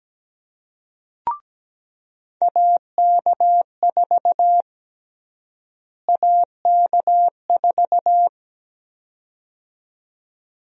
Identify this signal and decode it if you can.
Morse code, heard as a keyed tone, about 17 wpm, 700 Hz. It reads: AK4 AK4
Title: USA Call Sign Prefixes - Repeated 2x - 17wpm